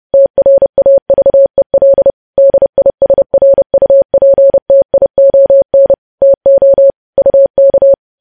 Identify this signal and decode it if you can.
Morse code, heard as a keyed tone, about 30 wpm, 550 Hz. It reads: TRAVEL DISRUPTION TO UK
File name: podcast / News headlines-1x-30x30